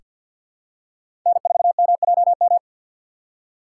Morse code, heard as a keyed tone, about 47 wpm, 700 Hz. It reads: N4MJM
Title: USA Call Signs - 5 Characters - 3x Character Spacing - 50wpm